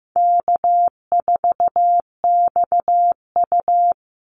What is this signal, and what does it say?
Morse code, heard as a keyed tone, about 15 wpm, 700 Hz. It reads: K4XU
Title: Easy Call Signs Repeated 2x - 15wpm